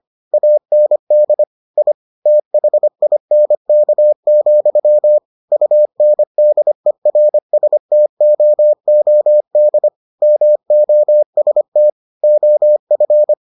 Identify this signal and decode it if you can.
Morse code, heard as a keyed tone, about 25 wpm, 600 Hz. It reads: AND I THINK, UNDERSTOOD MOST OF